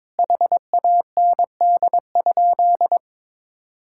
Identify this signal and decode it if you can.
Morse code, heard as a keyed tone, about 22 wpm, 700 Hz. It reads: HAND?
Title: Sentences from Top 1600 Words 22wpm